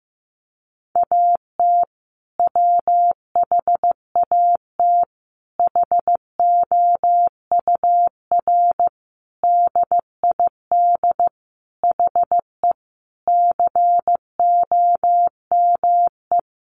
Morse code, heard as a keyed tone, about 15 wpm, 700 Hz. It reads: AT WHAT HOUR DID HE COME
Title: Sentences from Top 700 Words 15wpm